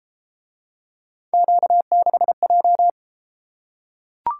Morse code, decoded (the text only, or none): Q6J